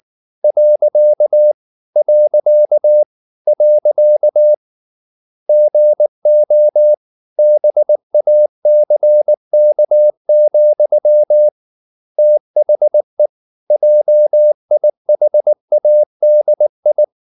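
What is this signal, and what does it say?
Morse code, heard as a keyed tone, about 19 wpm, 600 Hz. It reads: . . GO BACK, THE JIHADI
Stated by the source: Morse code at 19 wpm from CW med Gnister, reading Ken Follett's novel Never in English.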